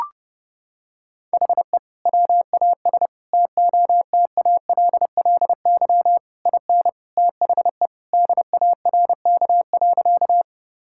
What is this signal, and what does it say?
Morse code, heard as a keyed tone, about 30 wpm, 700 Hz. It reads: HE WAS TOTALLY IN THE DARK.
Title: Sentences from Top 1600 Words 30wpm